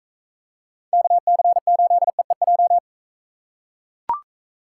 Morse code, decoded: KC8EEJ

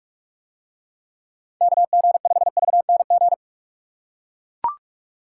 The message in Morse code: KG5VNG

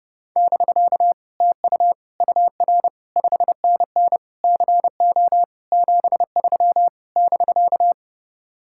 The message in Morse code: <BK> TU UR 5NN CO 73 <BK>

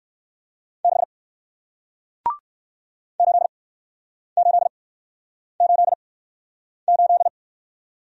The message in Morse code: 6 8888

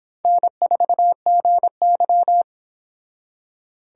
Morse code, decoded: N4GY

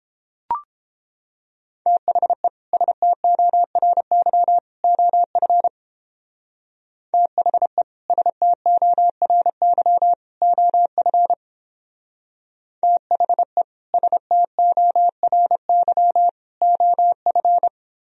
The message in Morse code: THE STORY OF THE STORY OF THE STORY OF

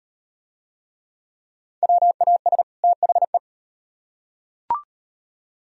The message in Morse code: WAS THE

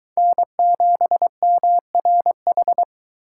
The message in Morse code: N7MRH

